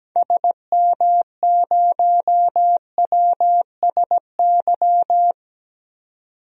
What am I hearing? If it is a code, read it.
Morse code, heard as a keyed tone, about 17 wpm, 700 Hz. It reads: SM0WSY
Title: DXCC-100 Call Signs - 6 Characters - 17wpm